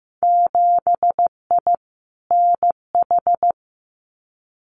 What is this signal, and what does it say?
Morse code, heard as a keyed tone, about 15 wpm, 700 Hz. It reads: 7I NH
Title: Winter Field Day Exchanges Encore 15wpm